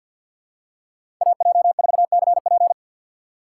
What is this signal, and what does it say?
Morse code, heard as a keyed tone, about 47 wpm, 700 Hz. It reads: AJ4CP
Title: USA Call Signs - 5 Characters - 8x Character Spacing - 50wpm